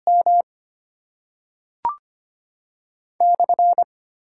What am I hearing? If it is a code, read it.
Morse code, heard as a keyed tone, about 25 wpm, 690 Hz. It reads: M /
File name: Top 90 QSO Elements - 8x Character Spacing - 25wpm